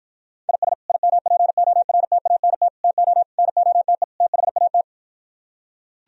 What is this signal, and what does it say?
Morse code, heard as a keyed tone, about 56 wpm, 700 Hz. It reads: IS IMPORTANT TO NOTE THAT